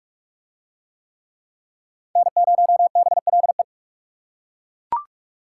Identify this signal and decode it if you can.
Morse code, heard as a keyed tone, about 40 wpm, 700 Hz. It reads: N0BLE